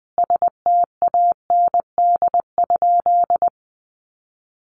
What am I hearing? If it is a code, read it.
Morse code, heard as a keyed tone, about 20 wpm, 700 Hz. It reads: STAND?
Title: Sentences from Top 300 Words 20wpm